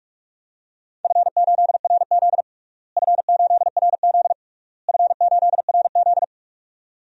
Morse code, heard as a keyed tone, about 46 wpm, 700 Hz. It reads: F8RZ F8RZ F8RZ
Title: DXCC-100 Call Signs - 4 Characters - Repeated 3x - 45wpm